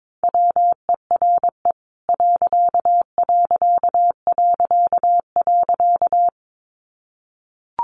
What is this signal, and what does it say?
Morse code, heard as a keyed tone, about 22 wpm, 700 Hz. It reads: WERE....
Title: A Sound Of Thunder 22wpm